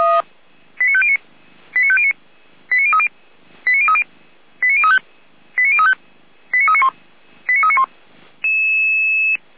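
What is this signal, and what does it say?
Five-tone selective call (ZVEI), heard as quick runs of five tones, tones of about 70 ms. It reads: 89509 89509 89030 89030 89035 89035 89391 89391
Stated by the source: ZVEI five-tone call 89062 received with an SDR.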